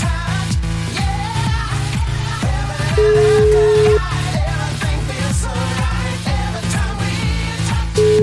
Ringback tone (Europe, 425 Hz), with noise.